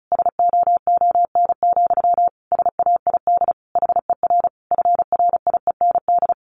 Morse code, heard as a keyed tone, about 34 wpm, 700 Hz. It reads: SOON, SAID HER FRIEND
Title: Sentences from Top 1500 Words 35wpm